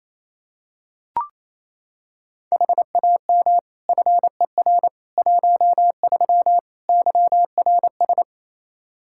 Morse code, heard as a keyed tone, about 28 wpm, 700 Hz. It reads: HAM FER 13 YRS
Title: Practice QSO - Exchange 3 - More Info - 28wpm